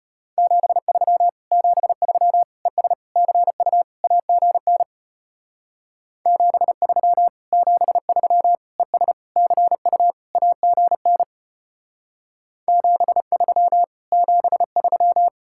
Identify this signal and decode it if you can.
Morse code, heard as a keyed tone, about 32 wpm, 700 Hz. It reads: 73 73 ES CU AGN 73 73 ES CU AGN 73 73